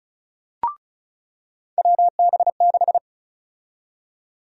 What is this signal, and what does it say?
Morse code, heard as a keyed tone, about 36 wpm, 700 Hz. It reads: WB6
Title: USA Call Sign Prefixes - 35wpm